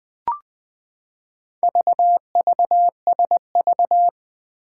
Morse code, heard as a keyed tone, about 20 wpm, 700 Hz. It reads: VVSV
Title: Four Challenging Letter-Number Combinations 20wpm